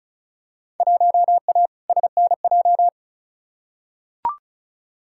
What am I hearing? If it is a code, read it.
Morse code, heard as a keyed tone, about 36 wpm, 700 Hz. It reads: 1A SNJ
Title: Field Day Exchanges - 35wpm